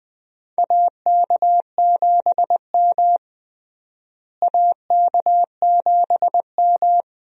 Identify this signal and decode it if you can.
Morse code, heard as a keyed tone, about 20 wpm, 700 Hz. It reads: AK7M AK7M
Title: USA Call Signs - 4 Characters - Repeated 2x - 20wpm